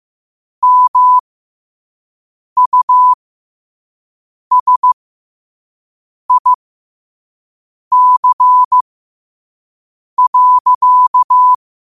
Morse code, heard as a keyed tone, about 15 wpm, spaced as for 5 wpm, 1000 Hz. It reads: MUSIC.
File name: qotdfiles / 5wpm